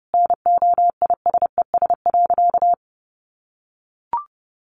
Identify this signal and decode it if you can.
Morse code, heard as a keyed tone, about 30 wpm, 700 Hz. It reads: NOISES.
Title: Easy Phrases 30wpm